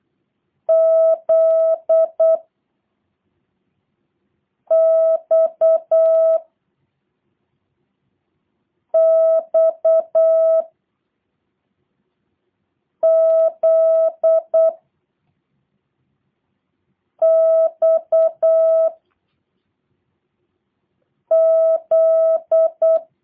Morse code, heard as a keyed tone, about 8 wpm, 650 Hz. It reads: Z X X Z X Z